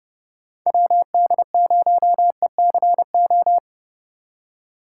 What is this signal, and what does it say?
Morse code, heard as a keyed tone, about 32 wpm, 700 Hz. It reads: WD0ECO
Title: K1USN SST QSO Practice - 30wpm